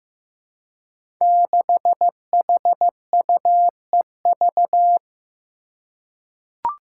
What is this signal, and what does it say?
Morse code, heard as a keyed tone, about 15 wpm, 700 Hz. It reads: 6HUEV
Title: Five Challenging Letter-Number Combinations 15wpm